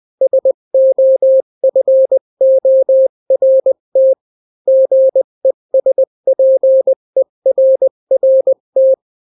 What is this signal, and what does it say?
Morse code, heard as a keyed tone, about 20 wpm, 530 Hz. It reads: SOFORT GESPERRT
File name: daily / 18wpm